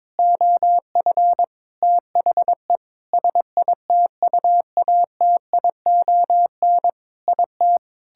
Morse code, heard as a keyed tone, about 22 wpm, 700 Hz. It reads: OF THE SITUATION IT
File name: Sentences with 9-Word Phrases - 22wpm